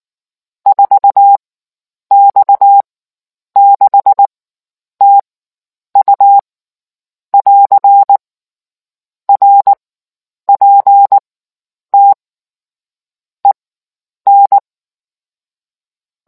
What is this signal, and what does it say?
Morse code, heard as a keyed tone, about 19 wpm, spaced as for 9 wpm, 800 Hz. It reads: 4X6TU <AR> RPT EN